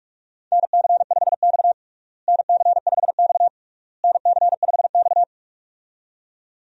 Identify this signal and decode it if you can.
Morse code, heard as a keyed tone, about 46 wpm, 700 Hz. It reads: NC5X NC5X NC5X
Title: USA Call Signs - 4 Characters - Repeated 3x - 45wpm